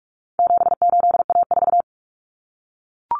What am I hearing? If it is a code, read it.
Morse code, heard as a keyed tone, about 44 wpm, 700 Hz. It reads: 78A4
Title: Common Cut Numbers - 4-Digit 45wpm